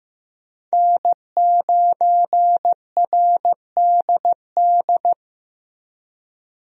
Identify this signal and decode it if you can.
Morse code, heard as a keyed tone, about 15 wpm, 700 Hz. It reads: N9RDD
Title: USA Call Signs - 5 Characters - 15wpm